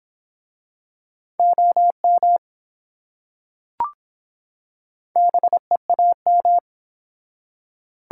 Morse code, heard as a keyed tone, about 24 wpm, 700 Hz. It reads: OM BEAM